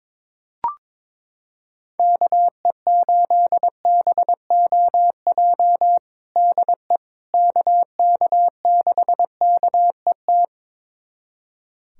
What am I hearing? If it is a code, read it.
Morse code, heard as a keyed tone, about 22 wpm, 700 Hz. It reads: KE8BOJ DE KK6KET
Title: USA Callsign DE Callsign - 6 Characters - 22wpm